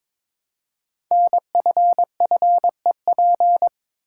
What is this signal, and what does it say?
Morse code, heard as a keyed tone, about 22 wpm, 700 Hz. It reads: NFFEP
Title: Code Groups - 5 Letters - 22wpm